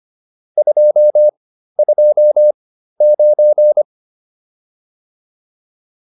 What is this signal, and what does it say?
Morse code, heard as a keyed tone, about 25 wpm, spaced as for 13 wpm, 600 Hz. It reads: 229
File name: QSO 407 13